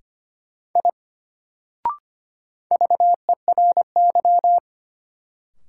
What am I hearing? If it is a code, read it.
Morse code, heard as a keyed tone, about 25 wpm, 700 Hz. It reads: I VERY
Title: Sentences from Top 200 Words - New Word Drill - 25wpm